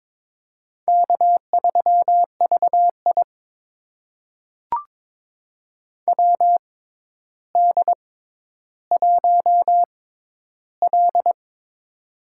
Morse code, heard as a keyed tone, about 22 wpm, spaced as for 11 wpm, 700 Hz. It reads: K3VI WD1L
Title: USA Call Signs - 4 Characters - 6x Character Spacing - 22wpm